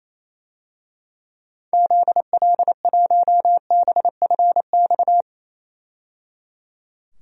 Morse code, heard as a keyed tone, about 27 wpm, 700 Hz. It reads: ZL1BFX
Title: DXCC-100 Call Signs - 6 Characters - 28wpm